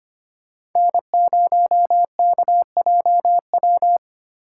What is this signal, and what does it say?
Morse code, heard as a keyed tone, about 25 wpm, 700 Hz. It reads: N0KJW
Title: USA Call Signs - 5 Characters - 4x Character Spacing - 25wpm